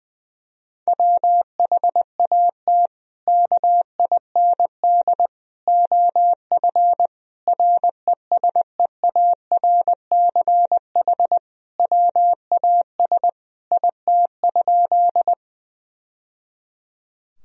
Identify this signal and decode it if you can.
Morse code, heard as a keyed tone, about 21 wpm, 700 Hz. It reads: WHAT KIND OF RESEARCH WAS IT?
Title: Sentences from Top 1000 Words - Review Part 4 - 20wpm